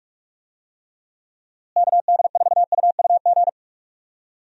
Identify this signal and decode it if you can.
Morse code, heard as a keyed tone, about 46 wpm, 700 Hz. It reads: KD4UUG
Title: USA Call Signs - 6 Characters - Repeated 2x - 45wpm